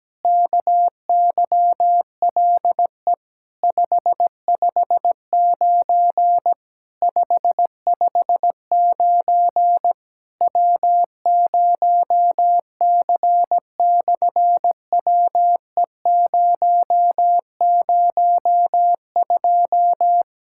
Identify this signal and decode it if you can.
Morse code, heard as a keyed tone, about 17 wpm, 700 Hz. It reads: KYLE 559 559 W0C/WE002
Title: SOTA QSO Practice 17wpm